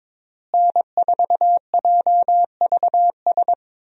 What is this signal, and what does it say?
Morse code, heard as a keyed tone, about 22 wpm, 700 Hz. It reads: N4JVS
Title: USA Call Signs - 5 Characters - Repeated 3x - 22wpm